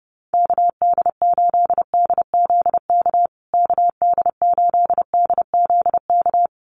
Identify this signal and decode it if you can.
Morse code, heard as a keyed tone, about 30 wpm, 700 Hz. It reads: KD8DZK KD8DZK
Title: USA Call Signs Repeated 2x 30wpm